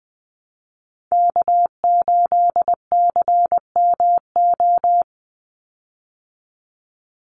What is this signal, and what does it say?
Morse code, heard as a keyed tone, about 20 wpm, 700 Hz. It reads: K8CMO
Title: Call Signs Repeated 3x - 20wpm